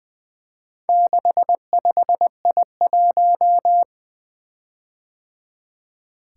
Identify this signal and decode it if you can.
Morse code, heard as a keyed tone, about 20 wpm, 700 Hz. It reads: 65I1
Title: Four Letter-Number Combinations - 20wpm